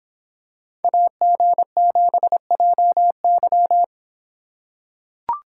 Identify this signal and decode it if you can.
Morse code, heard as a keyed tone, about 25 wpm, 700 Hz. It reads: AG7JY